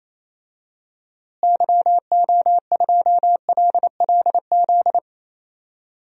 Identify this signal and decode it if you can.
Morse code, heard as a keyed tone, about 27 wpm, 700 Hz. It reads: YO2LLZ
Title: DXCC-100 Call Signs - 6 Characters - Repeated 2x - 28wpm